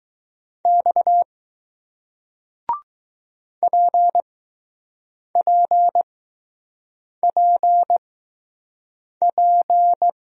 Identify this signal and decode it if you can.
Morse code, heard as a keyed tone, about 20 wpm, spaced as for 6 wpm, 700 Hz. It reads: X PPPP